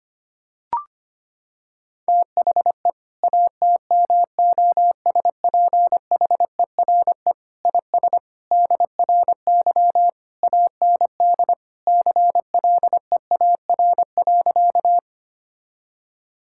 Morse code, heard as a keyed tone, about 25 wpm, 700 Hz. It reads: THE ATMOSPHERE IS DRY AND CLEAR.
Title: Sentences from Top 1600 Words 25wpm